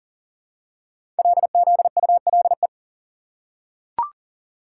Morse code, heard as a keyed tone, about 40 wpm, 700 Hz. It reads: LZULE
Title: Code Groups - 5 Letters - 40wpm